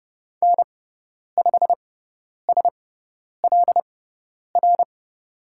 Morse code, heard as a keyed tone, about 30 wpm, spaced as for 10 wpm, 700 Hz. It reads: N5SLR
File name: Lesson 16g - Letter c - Call Signs 30wpm